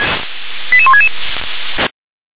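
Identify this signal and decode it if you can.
Five-tone selective call (ZVEI), heard as quick runs of five tones, tones of about 70 ms. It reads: 88166
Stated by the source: ZVEI five-tone selective call 88913 received via SDR.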